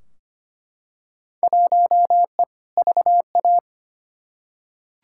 Morse code, heard as a keyed tone, about 25 wpm, 700 Hz. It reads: 1E VA
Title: Field Day Exchanges - 25wpm